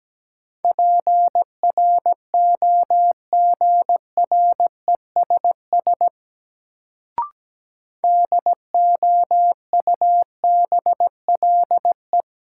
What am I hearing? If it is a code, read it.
Morse code, heard as a keyed tone, about 17 wpm, 700 Hz. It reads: PROGRESS DOUBLE
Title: Sentences from Top 1400 Words - New Word Drill Alt - 17wpm